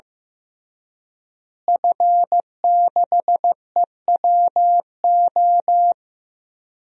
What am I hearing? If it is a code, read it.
Morse code, heard as a keyed tone, about 15 wpm, 700 Hz. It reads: F6EWO
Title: DXCC-100 Call Signs - 5 Characters - Repeated 3x - 15wpm